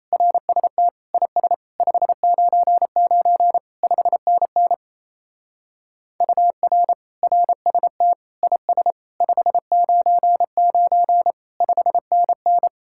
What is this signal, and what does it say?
Morse code, heard as a keyed tone, about 33 wpm, 700 Hz. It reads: RST IS 599 5NN UR RST IS 599 5NN
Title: Speed-Racing Common QSO Phrases - 28wpm